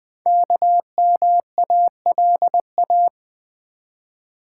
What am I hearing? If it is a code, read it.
Morse code, heard as a keyed tone, about 20 wpm, 700 Hz. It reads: KMALA
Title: Code Groups - 5 Letters - 20wpm